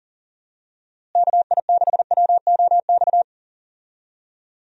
Morse code, heard as a keyed tone, about 37 wpm, 700 Hz. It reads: KI6WOX
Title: USA Call Signs - 6 Characters - Repeated 2x - 40wpm